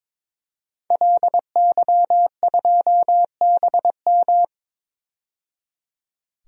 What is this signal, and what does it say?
Morse code, heard as a keyed tone, about 22 wpm, 700 Hz. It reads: LY2BM
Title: DXCC-100 Call Signs - 5 Characters - 22wpm